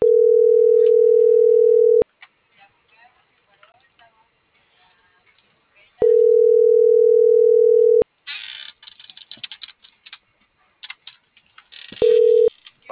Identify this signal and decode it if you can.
Ringback tone (North America), 440 + 480 Hz.